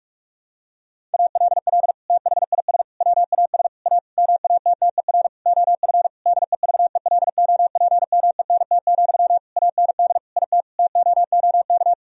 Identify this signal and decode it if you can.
Morse code, heard as a keyed tone, about 46 wpm, 700 Hz. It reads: ALL THIS WAS A MATTER OF DEVELOPMENT, AND IT TOOK
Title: Sentences from Top 1000 Words - Review Part 4 - 45wpm